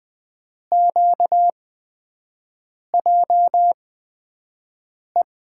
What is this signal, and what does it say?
Morse code, heard as a keyed tone, about 20 wpm, spaced as for 5 wpm, 700 Hz. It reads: QJE
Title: USA Call Signs - 6 Characters - 8x Character Spacing - 20wpm